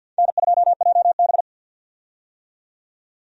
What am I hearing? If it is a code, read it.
Morse code, heard as a keyed tone, about 53 wpm, 700 Hz. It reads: N2JB
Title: USA Call Signs - 4 Characters - 50wpm